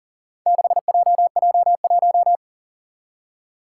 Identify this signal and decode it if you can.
Morse code, heard as a keyed tone, about 41 wpm, 700 Hz. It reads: 6JJ1
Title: Four Challenging Letter-Number Combinations 40wpm